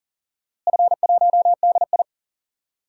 Morse code, heard as a keyed tone, about 40 wpm, 700 Hz. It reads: F1DI
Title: DXCC-100 Call Signs - 4 Characters - Repeated 3x - 40wpm